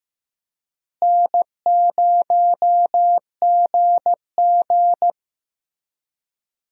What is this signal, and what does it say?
Morse code, heard as a keyed tone, about 15 wpm, 700 Hz. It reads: N0GG
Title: USA Call Signs - 4 Characters - Repeated 3x - 15wpm